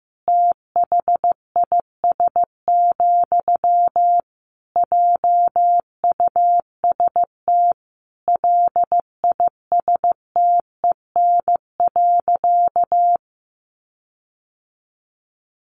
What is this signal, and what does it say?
Morse code, heard as a keyed tone, about 15 wpm, 700 Hz. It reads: THIS, JUST LISTEN.
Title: Sentences from Top 1300 Words 15wpm